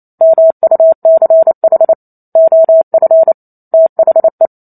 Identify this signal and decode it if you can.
Morse code, heard as a keyed tone, about 28 wpm, 650 Hz. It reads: MUCH OF THE
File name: Chapter 2 - 28 WPM